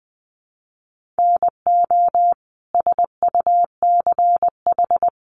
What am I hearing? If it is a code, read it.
Morse code, heard as a keyed tone, about 20 wpm, 700 Hz. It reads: NO SUCH
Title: Sentences from Top 1500 Words 20wpm